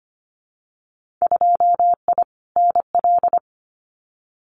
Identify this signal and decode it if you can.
Morse code, heard as a keyed tone, about 25 wpm, 700 Hz. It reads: 2I NL
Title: Winter Field Day Exchanges Encore 25wpm